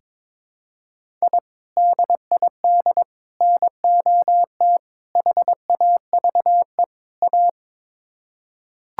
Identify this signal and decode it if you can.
Morse code, heard as a keyed tone, about 22 wpm, 700 Hz. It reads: I DID NOT HAVE A
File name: practice / 22wpm / Build A Phrase - 5-Word Phrases - 22wpm